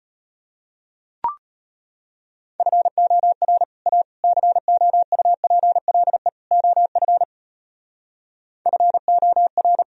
Morse code, heard as a keyed tone, about 38 wpm, 700 Hz. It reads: FOR A COUPLE OF FOR